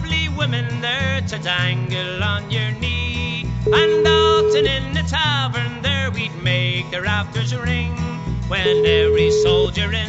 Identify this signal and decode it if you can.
Ringback tone (Europe, 425 Hz), with noise.